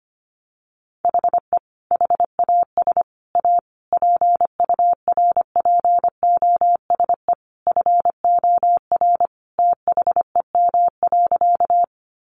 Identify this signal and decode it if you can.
Morse code, heard as a keyed tone, about 25 wpm, 700 Hz. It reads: HE HAS A PURPOSE FOR THEM.
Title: Sentences from Top 800 Words 25wpm